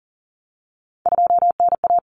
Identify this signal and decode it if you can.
Morse code, heard as a keyed tone, about 40 wpm, 700 Hz. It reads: 2NA
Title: Common Cut Numbers - 3-Digit 40wpm